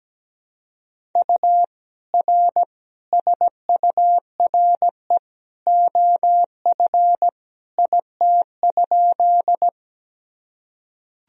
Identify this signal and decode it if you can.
Morse code, heard as a keyed tone, about 17 wpm, 700 Hz. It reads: U R SURE OF IT?
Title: Sentences from Top 300 Words and Abbreviations 17wpm